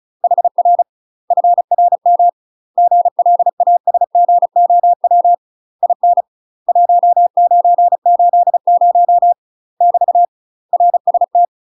Morse code, heard as a keyed tone, about 36 wpm, 700 Hz. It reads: HR FRM GLASGOW IN 1980 <BT> RST
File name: UK MASTER QSOs TRAINING / 35WPM / UK MASTER QSOs 0011